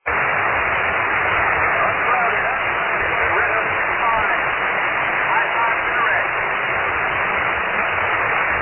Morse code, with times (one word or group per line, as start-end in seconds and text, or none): none